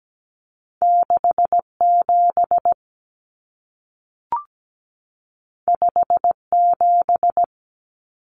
Morse code, heard as a keyed tone, about 17 wpm, 700 Hz. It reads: 67 57
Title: Common Cut Numbers - 2-Digit 17wpm